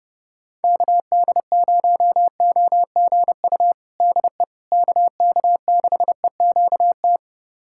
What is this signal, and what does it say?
Morse code, heard as a keyed tone, about 30 wpm, 700 Hz. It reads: KD0OGU DE KK6EQT
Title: USA Callsign DE Callsign - 6 Characters - 30wpm